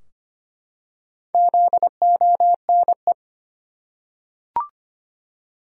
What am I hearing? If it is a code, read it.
Morse code, heard as a keyed tone, about 25 wpm, 700 Hz. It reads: ZONE
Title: Four Letter Words - 25wpm